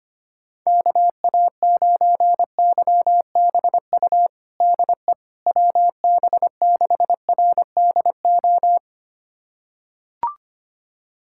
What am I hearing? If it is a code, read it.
Morse code, heard as a keyed tone, about 25 wpm, 700 Hz. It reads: KA9YBU DE WB6RDO
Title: USA Callsign DE Callsign 25wpm